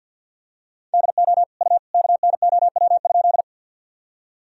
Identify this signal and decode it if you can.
Morse code, heard as a keyed tone, about 45 wpm, 700 Hz. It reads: DO U KNOW?
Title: Sentences from Top 100 Words and Abbreviations 50wpm